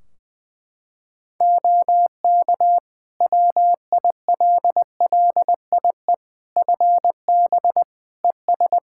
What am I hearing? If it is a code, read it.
Morse code, heard as a keyed tone, about 20 wpm, 700 Hz. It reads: OK WILLIE FB ES
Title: Practice QSO - Exchange 2 - Info - 20wpm